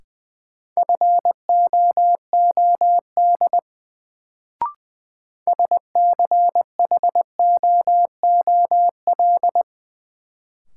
Morse code, heard as a keyed tone, about 20 wpm, 700 Hz. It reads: FOOD SCHOOL
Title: Sentences from Top 300 Words - New Word Drill - 20wpm